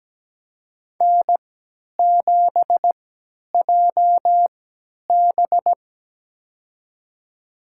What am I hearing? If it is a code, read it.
Morse code, heard as a keyed tone, about 17 wpm, spaced as for 10 wpm, 700 Hz. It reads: N7JB
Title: USA Call Signs - 4 Characters - 3x Character Spacing - 17wpm